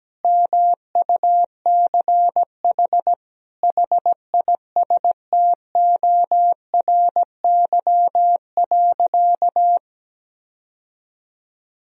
Morse code, heard as a keyed tone, about 17 wpm, 700 Hz. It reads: MUCH HISTORY.
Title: Sentences from Top 1000 Words - Review Part 2 - 17wpm